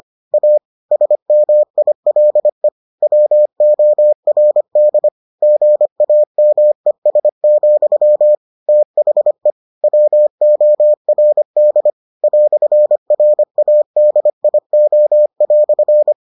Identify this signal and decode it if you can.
Morse code, heard as a keyed tone, about 25 wpm, 600 Hz. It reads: A SMILE WORD GAMES, THE WORD "RADIO"